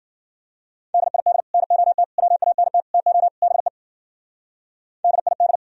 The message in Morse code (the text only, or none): DID NOT WANT TO BE DID